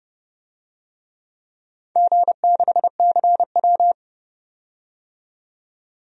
Morse code, heard as a keyed tone, about 32 wpm, 700 Hz. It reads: G6CW
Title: DXCC-100 Call Signs - 4 Characters - 30wpm